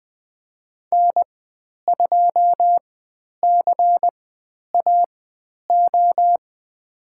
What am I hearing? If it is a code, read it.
Morse code, heard as a keyed tone, about 20 wpm, spaced as for 10 wpm, 700 Hz. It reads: N2CAO
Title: 102 Lesson 17g - Letter u - Call Signs 20wpm